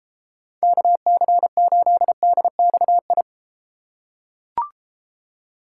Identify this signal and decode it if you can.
Morse code, heard as a keyed tone, about 33 wpm, 700 Hz. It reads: KC8DXI